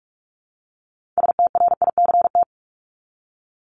Morse code, heard as a keyed tone, about 44 wpm, 700 Hz. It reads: STRICT
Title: Top 1501-1750 Words 45wpm